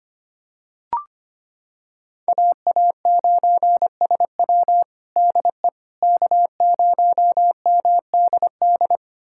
Morse code, heard as a keyed tone, about 25 wpm, 700 Hz. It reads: AA9SW DE K0MDD
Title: USA Callsign DE Callsign 25wpm